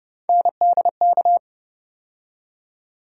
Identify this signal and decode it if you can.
Morse code, heard as a keyed tone, about 30 wpm, 700 Hz. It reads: NDK
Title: USA Call Sign Postfixes - 3 Letters 30wpm